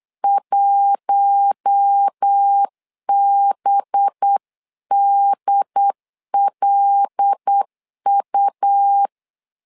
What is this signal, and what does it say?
Morse code, heard as a keyed tone, about 8 wpm, 800 Hz. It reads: 1BDLU
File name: LEZIONI / 5 LBVC / 5 LBVC 8wpm 2935